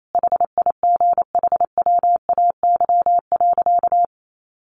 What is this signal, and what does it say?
Morse code, heard as a keyed tone, about 28 wpm, 700 Hz. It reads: HIGHWAY.
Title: Sentences from Top 1400 Words 28wpm